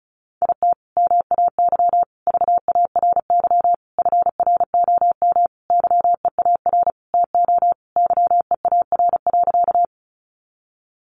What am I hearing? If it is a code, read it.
Morse code, heard as a keyed tone, about 34 wpm, 700 Hz. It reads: IT MAY VARY FROM YEAR TO YEAR.
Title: Sentences from Top 1200 Words 35wpm